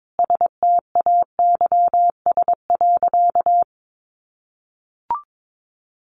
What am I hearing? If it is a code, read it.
Morse code, heard as a keyed tone, about 22 wpm, 700 Hz. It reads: STAYS.
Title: A Sound Of Thunder 22wpm